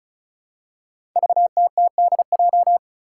Morse code, heard as a keyed tone, about 34 wpm, 700 Hz. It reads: VTTDJ
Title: Five Challenging Letter-Number Combinations 35wpm